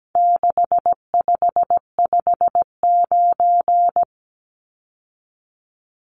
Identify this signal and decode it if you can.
Morse code, heard as a keyed tone, about 17 wpm, 700 Hz. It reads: 6559
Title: Numbers - 4-Digit 17wpm